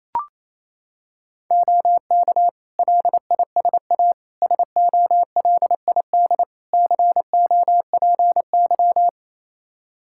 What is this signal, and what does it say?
Morse code, heard as a keyed tone, about 28 wpm, 700 Hz. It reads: OK LISA SOLID COPY
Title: Practice QSO - Exchange 3 - More Info - 28wpm